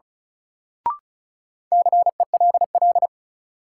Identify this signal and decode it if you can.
Morse code, heard as a keyed tone, about 36 wpm, 700 Hz. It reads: CELL
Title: Sentences from Top 1100 Words - New Word Drill Alt - 35wpm